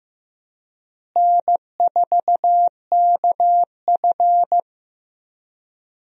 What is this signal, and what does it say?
Morse code, heard as a keyed tone, about 15 wpm, 700 Hz. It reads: N4KF
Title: USA Call Signs - 4 Characters - Repeated 2x - 15wpm